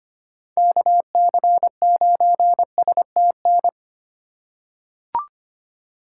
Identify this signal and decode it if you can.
Morse code, heard as a keyed tone, about 25 wpm, 700 Hz. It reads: KC9STN
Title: USA Call Signs - 6 Characters - 4x Character Spacing - 25wpm